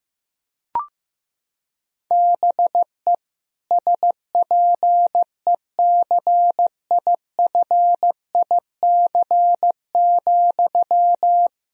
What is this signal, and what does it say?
Morse code, heard as a keyed tone, about 16 wpm, 700 Hz. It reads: BE SPECIFIC,
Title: Sentences from Top 1000 Words - Review Part 3 - 15wpm